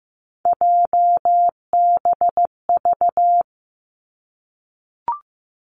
Morse code, heard as a keyed tone, about 15 wpm, 700 Hz. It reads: JBV
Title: Three Challenging Letter-Number Combinations 15wpm